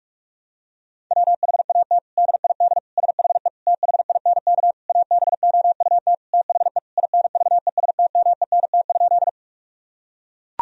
Morse code, heard as a keyed tone, about 44 wpm, 700 Hz. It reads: WHAT DID SHE THINK ABOUT THE INVESTMENT?
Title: Sentences from Top 1000 Words - Review Part 4 - 45wpm